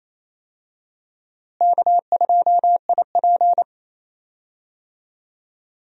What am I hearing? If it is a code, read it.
Morse code, heard as a keyed tone, about 28 wpm, 700 Hz. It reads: K2IP
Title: USA Call Signs - 4 Characters - Repeated 3x - 28wpm